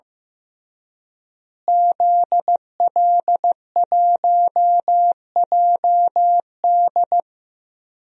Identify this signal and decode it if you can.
Morse code, heard as a keyed tone, about 16 wpm, 700 Hz. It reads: ZL1JD
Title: DXCC-100 Call Signs - 5 Characters - Repeated 3x - 15wpm